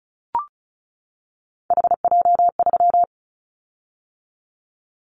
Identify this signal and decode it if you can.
Morse code, heard as a keyed tone, about 34 wpm, 700 Hz. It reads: HJ3
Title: Callsign Prefixes for Top 100 DXCC - 35wpm